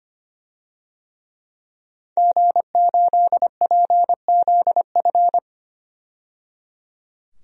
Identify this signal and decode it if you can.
Morse code, heard as a keyed tone, about 25 wpm, 700 Hz. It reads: G8PZF
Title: DXCC-100 Call Signs - 5 Characters - 25wpm